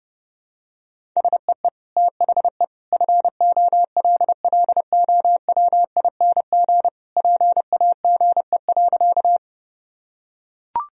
Morse code, heard as a keyed tone, about 30 wpm, 700 Hz. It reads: SEE THE FOLLOWING PAGE.
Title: Sentences from Top 1000 Words - Review Part 1 - 30wpm